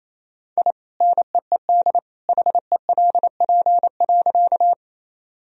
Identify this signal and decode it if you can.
Morse code, heard as a keyed tone, about 28 wpm, 700 Hz. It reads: I NEED HELP.
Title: Sentences from Top 1000 Words - Review Part 1 - 28wpm